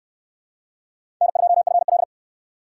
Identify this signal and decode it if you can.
Morse code, heard as a keyed tone, about 66 wpm, 700 Hz. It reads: N2FL